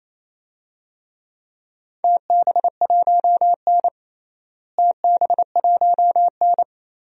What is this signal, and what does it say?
Morse code, heard as a keyed tone, about 28 wpm, 700 Hz. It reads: TB1N TB1N
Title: DXCC-100 Call Signs - 4 Characters - Repeated 2x - 28wpm